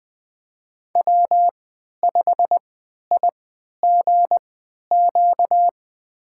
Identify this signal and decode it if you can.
Morse code, heard as a keyed tone, about 20 wpm, spaced as for 11 wpm, 700 Hz. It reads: W5IGQ
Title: USA Call Signs - 5 Characters - 3x Character Spacing - 20wpm